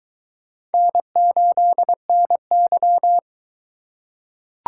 Morse code, heard as a keyed tone, about 22 wpm, 700 Hz. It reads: N8NY